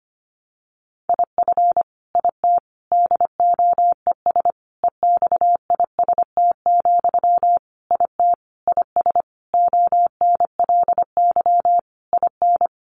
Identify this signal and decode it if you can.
Morse code, heard as a keyed tone, about 25 wpm, 700 Hz. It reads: IF IT DOES EXIST, IT IS ONLY IN
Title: Sentences from Top 1300 Words 25wpm